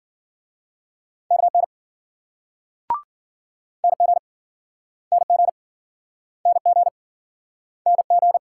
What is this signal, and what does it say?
Morse code, heard as a keyed tone, about 52 wpm, 700 Hz. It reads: KN NG NG NG NG